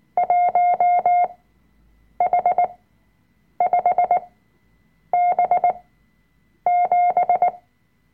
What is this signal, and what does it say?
Morse code, heard as a keyed tone, about 19 wpm, spaced as for 7 wpm, 710 Hz. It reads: 1H5B7
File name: Workshop Dateien / CW-Online-Kurs / Lektion 005